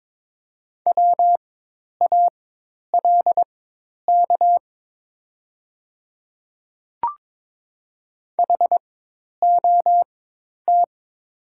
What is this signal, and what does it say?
Morse code, heard as a keyed tone, about 22 wpm, spaced as for 10 wpm, 700 Hz. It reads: WALK HOT
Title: 4x Character Spacing - Top 300 Words - 22wpm